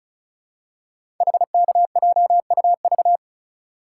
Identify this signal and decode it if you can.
Morse code, heard as a keyed tone, about 36 wpm, 690 Hz. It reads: HKJUV
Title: Code Groups - 5 Letters - 35wpm